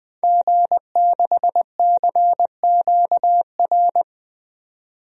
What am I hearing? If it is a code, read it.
Morse code, heard as a keyed tone, about 20 wpm, 700 Hz. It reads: G6CQR
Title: DXCC-100 Call Signs - 5 Characters - 20wpm